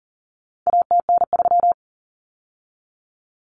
Morse code, heard as a keyed tone, about 40 wpm, 700 Hz. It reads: ATN3
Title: Common Cut Numbers - 4-Digit 40wpm